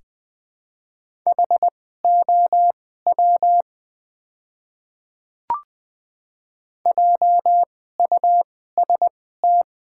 Morse code, heard as a keyed tone, about 20 wpm, spaced as for 14 wpm, 700 Hz. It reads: HOW JUST
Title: 2x Character Spacing - Top 300 Words - 20wpm